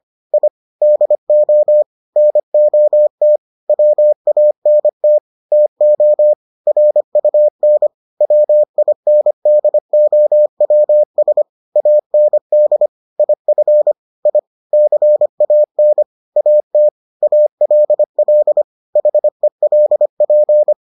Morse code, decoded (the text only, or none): I DO NOT WANT TO RUN WINDOWS AND IF I CAN AT ALL HELP